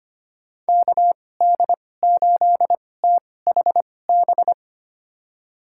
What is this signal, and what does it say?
Morse code, heard as a keyed tone, about 25 wpm, spaced as for 18 wpm, 700 Hz. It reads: KD8THB
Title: USA Call Signs - 6 Characters - 2x Character Spacing - 25wpm